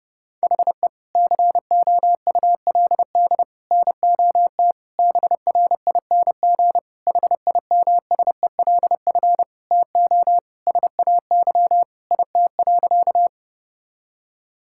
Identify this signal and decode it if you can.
Morse code, heard as a keyed tone, about 32 wpm, 700 Hz. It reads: HE COULD NOT BRING HIMSELF TO SAY IT.
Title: Sentences from Top 1600 Words 30wpm